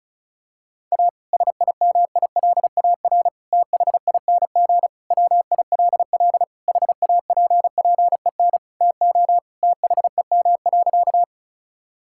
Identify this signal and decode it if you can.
Morse code, heard as a keyed tone, about 36 wpm, 700 Hz. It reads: A SIMILAR THING WILL HAPPEN TO THEM.
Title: Sentences from Top 1000 Words - Review Part 5 - 35wpm